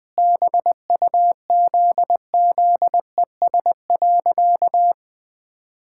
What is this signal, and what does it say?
Morse code, heard as a keyed tone, about 20 wpm, 700 Hz. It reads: BUZZES.
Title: Sentences with 4-Word Phrases - Spoken Beginning - 20wpm